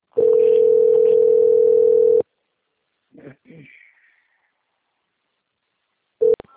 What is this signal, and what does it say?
Ringback tone (North America), 440 + 480 Hz.
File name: out- - - - - .7506